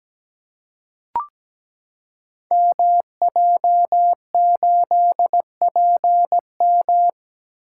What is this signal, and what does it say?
Morse code, heard as a keyed tone, about 17 wpm, 700 Hz. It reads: MJ8PM
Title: Code Groups - 5 Letter:Numbers - 17wpm